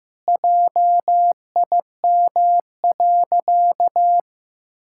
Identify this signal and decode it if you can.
Morse code, heard as a keyed tone, about 16 wpm, 700 Hz. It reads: JIM.
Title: Pangram Challenge v2 - 15wpm